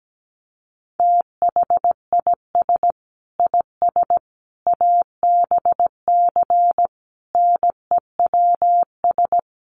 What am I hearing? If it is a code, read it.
Morse code, heard as a keyed tone, about 17 wpm, 700 Hz. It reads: THIS IS ABC NEWS
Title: Sets of 5 Words 17wpm